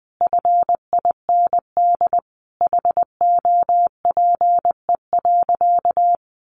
Morse code, heard as a keyed tone, about 20 wpm, 700 Hz. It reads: FIND HOPE.
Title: Sentences from Top 1300 Words 20wpm